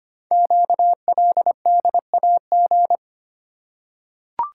Morse code, heard as a keyed tone, about 26 wpm, 700 Hz. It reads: QLDAG
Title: Code Groups - 5 Letters - 25wpm